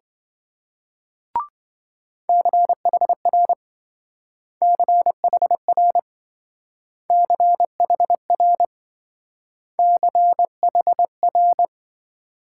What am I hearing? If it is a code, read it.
Morse code, heard as a keyed tone, about 26 wpm, 700 Hz. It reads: CHR CHR CHR CHR